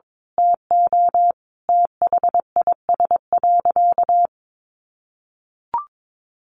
Morse code, heard as a keyed tone, about 22 wpm, 700 Hz. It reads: TO THIS.
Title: Sentences from Top 1400 Words 22wpm